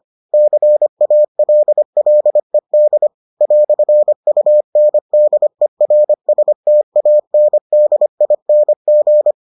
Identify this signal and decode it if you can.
Morse code, heard as a keyed tone, about 25 wpm, 600 Hz. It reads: CALLED "UNDERSTANDING